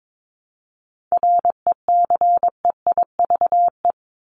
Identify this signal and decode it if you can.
Morse code, heard as a keyed tone, about 22 wpm, 700 Hz. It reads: RECEIVE
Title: Top 751-1000 Words 22wpm